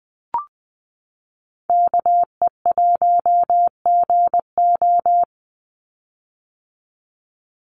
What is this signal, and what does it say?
Morse code, heard as a keyed tone, about 20 wpm, 700 Hz. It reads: KE1GO
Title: USA Call Signs 20wpm